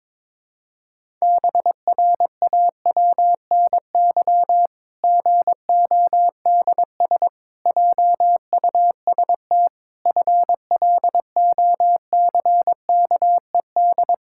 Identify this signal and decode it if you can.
Morse code, heard as a keyed tone, about 22 wpm, 700 Hz. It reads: BRAWNY GODS JUST FLOCKED
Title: Pangram Challenge v2 - 22wpm